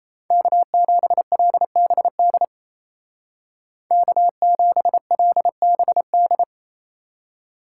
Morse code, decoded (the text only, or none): K7LBD K7LBD